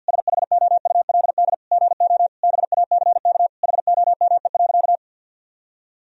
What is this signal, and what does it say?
Morse code, heard as a keyed tone, about 50 wpm, 700 Hz. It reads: SHOULD GO BACK HOME.
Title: Sentences from Top 1000 Words - Review Part 4 - 50wpm